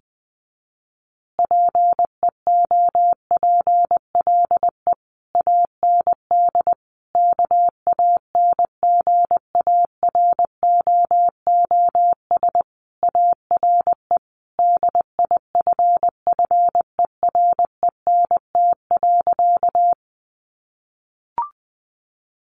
Morse code, heard as a keyed tone, about 20 wpm, 700 Hz. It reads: PEOPLE AND KANGAROOS ARE DIFFERENT.
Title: Easy Phrases 20wpm